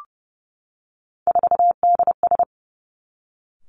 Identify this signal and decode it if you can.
Morse code, heard as a keyed tone, about 30 wpm, 700 Hz. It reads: 4DS
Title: Three Challenging Letter-Number Combinations 30wpm